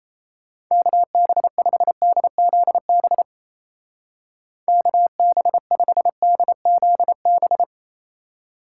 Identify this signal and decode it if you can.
Morse code, heard as a keyed tone, about 33 wpm, 700 Hz. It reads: KB5DZB KB5DZB
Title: Speed Racing - USA 2x3 Call Signs 28wpm